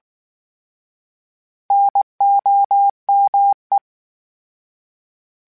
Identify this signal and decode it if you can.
Morse code, heard as a keyed tone, about 19 wpm, 800 Hz. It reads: NOME